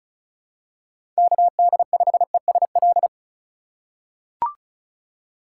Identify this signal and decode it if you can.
Morse code, heard as a keyed tone, about 36 wpm, 700 Hz. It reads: KD5ESL
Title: USA Call Signs - 6 Characters - 3x Character Spacing - 35wpm